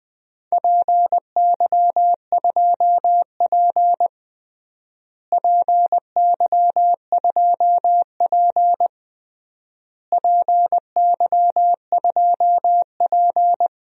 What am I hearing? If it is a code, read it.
Morse code, heard as a keyed tone, about 20 wpm, 700 Hz. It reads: PY2P PY2P PY2P
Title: DXCC-100 Call Signs - 4 Characters - Repeated 3x - 20wpm